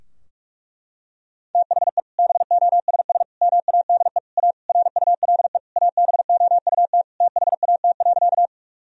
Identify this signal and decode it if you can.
Morse code, heard as a keyed tone, about 45 wpm, 700 Hz. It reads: THE BOSS MADE A RULE ABOUT THAT.
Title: Sentences from Top 1000 Words - Review Part 4 - 45wpm